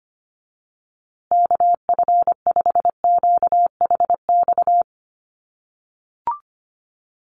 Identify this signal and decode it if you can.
Morse code, heard as a keyed tone, about 25 wpm, 700 Hz. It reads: KF5QHX
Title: USA Call Signs Repeated 2x 25wpm